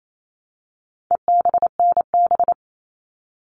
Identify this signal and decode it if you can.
Morse code, heard as a keyed tone, about 28 wpm, 700 Hz. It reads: EBNB
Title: All Cut Numbers - 4-Digit 28wpm